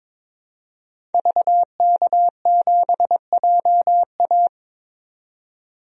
Morse code, heard as a keyed tone, about 22 wpm, 700 Hz. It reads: VK7JA
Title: DXCC-100 Call Signs - 5 Characters - 22wpm